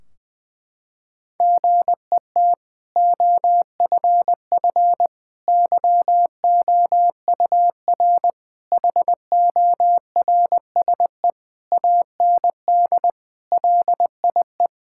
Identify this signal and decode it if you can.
Morse code, heard as a keyed tone, about 20 wpm, 700 Hz. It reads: GET OFF YOUR HORSE AND LIE
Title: Sentences from Top 1600 Words 20wpm